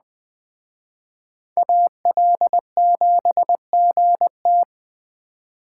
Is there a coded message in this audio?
Yes: Morse code from 1.6 to 4.6 s, heard as a keyed tone, about 20 wpm, 700 Hz. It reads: AL7GT